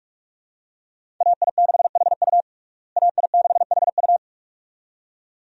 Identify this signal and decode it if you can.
Morse code, heard as a keyed tone, about 46 wpm, 700 Hz. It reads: AI6HU AI6HU
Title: USA Call Signs - 5 Characters - Repeated 2x - 45wpm